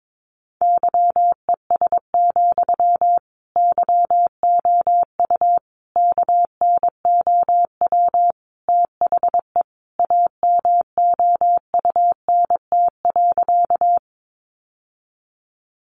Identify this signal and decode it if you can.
Morse code, heard as a keyed tone, about 22 wpm, 700 Hz. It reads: YES, YOU KNOW THE AMOUNT.
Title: Sentences from Top 700 Words 22wpm